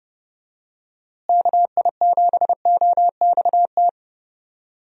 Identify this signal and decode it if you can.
Morse code, heard as a keyed tone, about 31 wpm, 700 Hz. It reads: KI7OXT
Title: USA Call Signs - 6 Characters - Repeated 2x - 30wpm